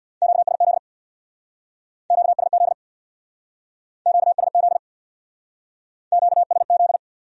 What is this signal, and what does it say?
Morse code, heard as a keyed tone, about 63 wpm, 700 Hz. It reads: QSZ QSZ QSZ QSZ